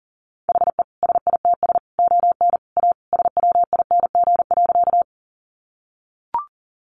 Morse code, heard as a keyed tone, about 40 wpm, 700 Hz. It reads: HE SITS ON A SWING.
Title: Additional Easy Phrases 40wpm